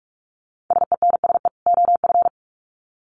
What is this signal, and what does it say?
Morse code, heard as a keyed tone, about 44 wpm, 700 Hz. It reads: SENSE OF
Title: Sets of 2 Words 45wpm